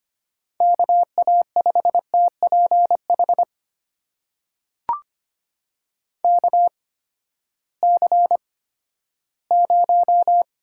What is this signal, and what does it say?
Morse code, heard as a keyed tone, about 25 wpm, spaced as for 14 wpm, 700 Hz. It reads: KA5TPH KC0
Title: USA Call Signs - 6 Characters - 8x Character Spacing - 25wpm